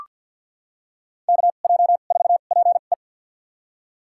Morse code, heard as a keyed tone, about 46 wpm, spaced as for 35 wpm, 700 Hz. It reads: KJ4PE
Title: USA Call Signs - 5 Characters - 2x Character Spacing - 50wpm